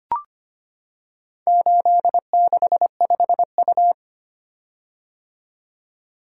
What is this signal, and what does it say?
Morse code, heard as a keyed tone, about 25 wpm, 700 Hz. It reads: 865U
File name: practice / 25wpm / Four Letter-Number Combinations - 25wpm